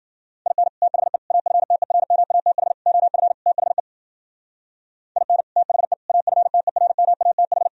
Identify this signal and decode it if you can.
Morse code, heard as a keyed tone, about 52 wpm, 700 Hz. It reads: IN THE AFTERMATH OF THE IN THE AFTERMATH